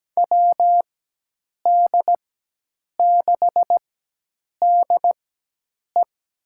Morse code, heard as a keyed tone, about 17 wpm, spaced as for 8 wpm, 700 Hz. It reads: WD6DE
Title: USA Call Signs - 6 Characters - 4x Character Spacing - 17wpm